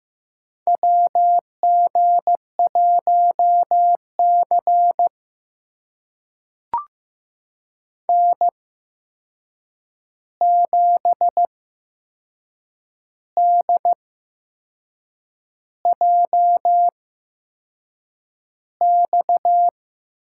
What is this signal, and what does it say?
Morse code, heard as a keyed tone, about 15 wpm, spaced as for 6 wpm, 700 Hz. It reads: WG1C N7DJX
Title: USA Call Signs - 8x Character Spacing - 15wpm